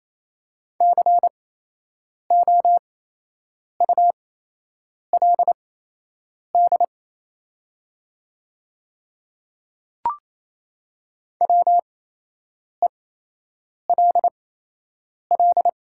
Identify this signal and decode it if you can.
Morse code, heard as a keyed tone, about 28 wpm, spaced as for 8 wpm, 700 Hz. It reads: COULD WELL
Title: 8x Character Spacing - Top 200 Words - 28wpm